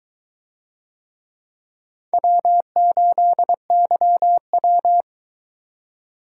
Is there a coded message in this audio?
Yes: Morse code from 2.1 to 5.0 s, heard as a keyed tone, about 23 wpm, 700 Hz. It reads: W8YW